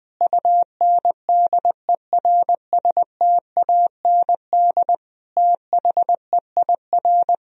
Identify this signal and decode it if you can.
Morse code, heard as a keyed tone, about 21 wpm, 700 Hz. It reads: UNDERSTAND THEIR
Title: Sentences from Top 1000 Words - Review Part 3 - 20wpm